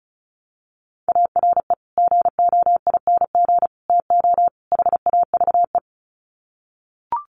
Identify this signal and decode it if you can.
Morse code, heard as a keyed tone, about 34 wpm, 700 Hz. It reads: ARE GOING TO HAVE
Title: Sets of 4 Words 35wpm